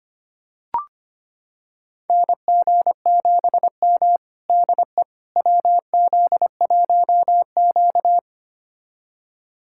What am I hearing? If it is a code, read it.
Morse code, heard as a keyed tone, about 25 wpm, 700 Hz. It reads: NG7M DE WZ1Q
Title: USA Callsign DE Callsign - 4 Characters - 25wpm